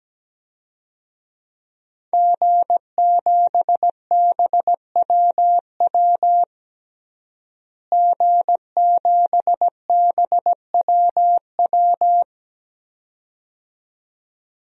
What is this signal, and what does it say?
Morse code, heard as a keyed tone, about 17 wpm, 700 Hz. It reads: G7BWW G7BWW
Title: DXCC-100 Call Signs - 5 Characters - Repeated 2x - 17wpm